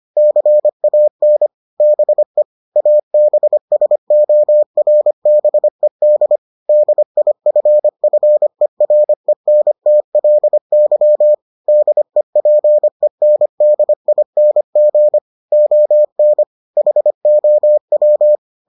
Morse code, heard as a keyed tone, about 25 wpm, 600 Hz. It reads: CAN BE ABSORBED DIFFERENTLY DEPENDING ON HOW